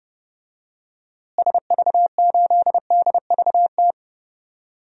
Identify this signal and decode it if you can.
Morse code, heard as a keyed tone, about 30 wpm, 700 Hz. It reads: SV8DVT
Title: DXCC-100 Call Signs - 6 Characters - Repeated 3x - 30wpm